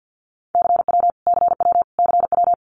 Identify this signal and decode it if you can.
Morse code, heard as a keyed tone, about 50 wpm, 700 Hz. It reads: CW CW CW
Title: Abbreviations Repeated 3x - 50wpm